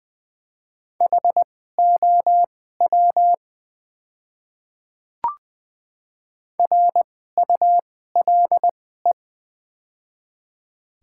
Morse code, heard as a keyed tone, about 21 wpm, spaced as for 15 wpm, 700 Hz. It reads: HOW RULE
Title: 2x Character Spacing - Top 500 Words - 20wpm